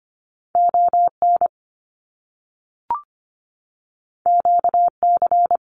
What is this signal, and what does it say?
Morse code, heard as a keyed tone, about 25 wpm, 700 Hz. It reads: ON QC
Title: Canadian Provinces and Territories 25wpm